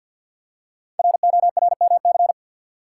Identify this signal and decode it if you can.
Morse code, heard as a keyed tone, about 47 wpm, 700 Hz. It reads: RORMC
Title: Code Groups - 5 Letters - 50wpm